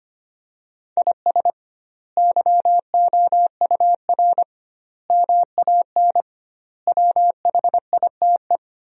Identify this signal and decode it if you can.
Morse code, heard as a keyed tone, about 25 wpm, 700 Hz. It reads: IS YOUR MAN WHITE
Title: 2x Word-Spacing - Sentences from Top 400 Words - 25wpm